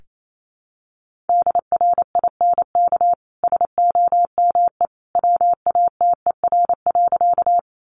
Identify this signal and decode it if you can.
Morse code, heard as a keyed tone, about 28 wpm, 700 Hz. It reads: DRINK SOME WATER.
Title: Sentences from Top 1500 Words 28wpm